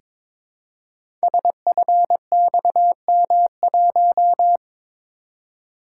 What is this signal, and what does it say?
Morse code, heard as a keyed tone, about 23 wpm, 700 Hz. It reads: SFXM1
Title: Code Groups - 5 Letter:Numbers - 22wpm